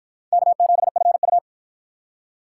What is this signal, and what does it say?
Morse code, heard as a keyed tone, about 48 wpm, 700 Hz. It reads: K7FU